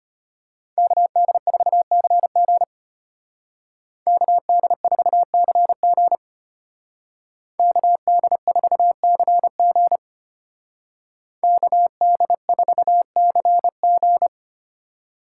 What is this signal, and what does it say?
Morse code, heard as a keyed tone, about 33 wpm, 700 Hz. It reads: KD4CG KD4CG KD4CG KD4CG